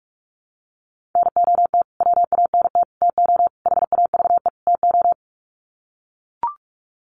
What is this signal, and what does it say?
Morse code, heard as a keyed tone, about 44 wpm, 700 Hz. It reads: NOT WANT TO HAVE TO
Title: Sets of 5 Words Encore2 45wpm